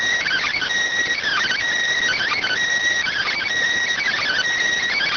Emergency Alert System (SAME) tones (header bursts), heard as harsh buzzing bursts. It is heard from 0.2 to 4.4 s.